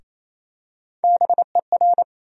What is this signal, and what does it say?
Morse code, heard as a keyed tone, about 28 wpm, 700 Hz. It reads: BER
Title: USA Call Sign Postfixes - 3 Letters 28wpm